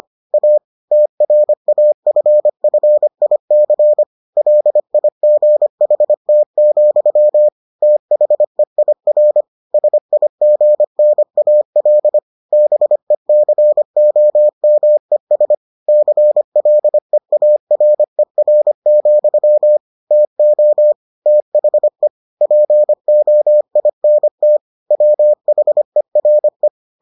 Morse code, heard as a keyed tone, about 25 wpm, 600 Hz. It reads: A TRAFFIC LIGHT, THEIR SIGNAL BECOMES CLEARER, TO THE POINT WHERE